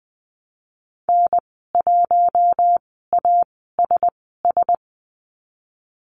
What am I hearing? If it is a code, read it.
Morse code, heard as a keyed tone, about 19 wpm, spaced as for 14 wpm, 700 Hz. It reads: N1ASS
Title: 048 Lesson 9g - Number 4 - Call Signs 20wpm